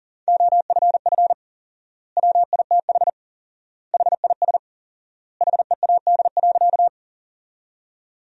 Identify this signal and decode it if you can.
Morse code, heard as a keyed tone, about 40 wpm, 700 Hz. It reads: OFF WITH HIS HEAD.
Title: 4x Word-Spacing - Sentences from Top 300 Words - 40wpm